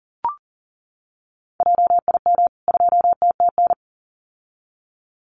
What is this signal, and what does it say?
Morse code, heard as a keyed tone, about 40 wpm, 700 Hz. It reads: JIM 2TTN
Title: CWT Exchange 40wpm